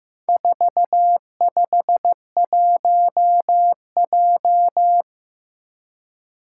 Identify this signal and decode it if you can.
Morse code, heard as a keyed tone, about 15 wpm, 700 Hz. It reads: 451J
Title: Four Challenging Letter-Number Combinations 15wpm